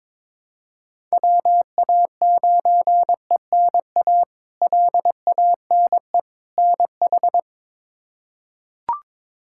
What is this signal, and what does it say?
Morse code, heard as a keyed tone, about 21 wpm, 700 Hz. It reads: WA9ENA LANE NH
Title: K1USN SST QSO Practice - 22wpm